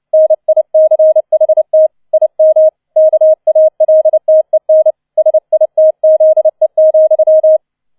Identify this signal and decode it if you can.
Morse code, heard as a keyed tone, about 29 wpm, 630 Hz. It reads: NICHT IM KALTEN SITZE,